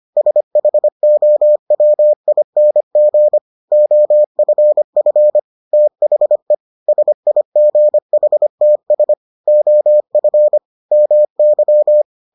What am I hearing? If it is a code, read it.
Morse code, heard as a keyed tone, about 25 wpm, 600 Hz. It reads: SHOWING OFF THE SIGHTS OF MY